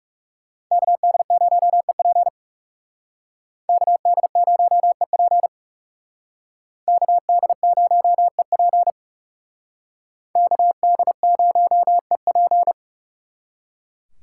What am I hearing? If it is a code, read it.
Morse code, heard as a keyed tone, about 38 wpm, 700 Hz. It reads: KD0EP KD0EP KD0EP KD0EP